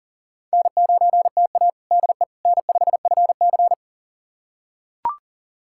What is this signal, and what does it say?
Morse code, heard as a keyed tone, about 37 wpm, 700 Hz. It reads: N9TA DE N5FC
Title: USA Callsign DE Callsign - 4 Characters - 40wpm